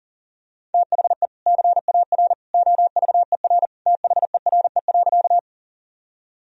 Morse code, heard as a keyed tone, about 40 wpm, 700 Hz. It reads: THE CAR OVER THERE.
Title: Sentences from Top 1000 Words - Review Part 1 - 40wpm